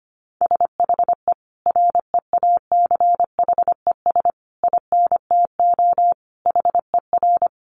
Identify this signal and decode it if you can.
Morse code, heard as a keyed tone, about 25 wpm, 700 Hz. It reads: SHE REACHES INTO HER
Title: Additional Easy Phrases 25wpm